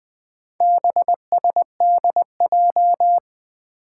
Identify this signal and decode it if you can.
Morse code, heard as a keyed tone, about 20 wpm, 700 Hz. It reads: BSDJ
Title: Four Challenging Letter-Number Combinations 20wpm